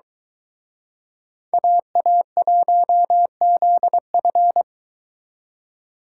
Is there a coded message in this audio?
Yes: Morse code from 1.5 to 4.6 s, heard as a keyed tone, about 23 wpm, 700 Hz. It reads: AA1ZF